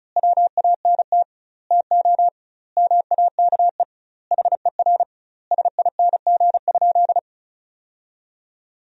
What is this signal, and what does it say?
Morse code, heard as a keyed tone, about 34 wpm, 700 Hz. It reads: WANT TO MAKE HER SING?
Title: 2x Word-Spacing - Sentences from Top 400 Words - 35wpm